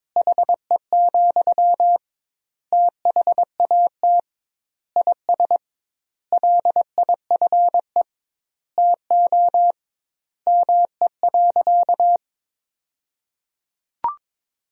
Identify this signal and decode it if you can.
Morse code, heard as a keyed tone, about 22 wpm, 700 Hz. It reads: HE, THAT IS LIFE TO ME.
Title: 2x Word-Spacing - Sentences from Top 500 Words - 22wpm